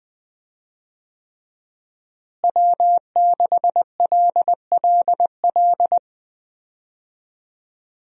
Morse code, heard as a keyed tone, about 20 wpm, 700 Hz. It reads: W6LLL
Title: USA Call Signs - 5 Characters - Repeated 3x - 20wpm